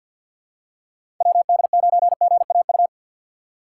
Morse code, heard as a keyed tone, about 49 wpm, 700 Hz. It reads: WD9GAU
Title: USA Call Signs - 6 Characters - 3x Character Spacing - 50wpm